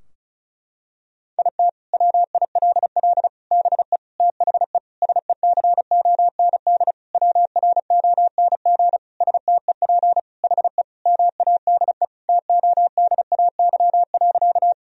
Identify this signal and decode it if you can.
Morse code, heard as a keyed tone, about 36 wpm, 700 Hz. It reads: IT WILL BE THE SECOND WRONG STEP HE MADE TODAY.
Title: Sentences from Top 1000 Words - Review Part 2 - 35wpm